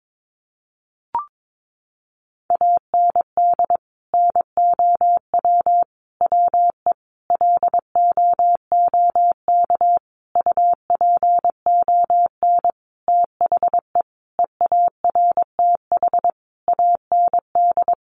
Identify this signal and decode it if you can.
Morse code, heard as a keyed tone, about 22 wpm, 700 Hz. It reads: AND NOW WE LOOK UPON THE EARTH AND
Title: Sentences from Top 1200 Words 22wpm